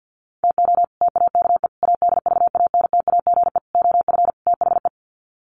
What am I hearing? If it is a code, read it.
Morse code, heard as a keyed tone, about 48 wpm, 700 Hz. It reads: TO TAKE ADVANTAGE OF THE
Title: Sets of 5 Words 50wpm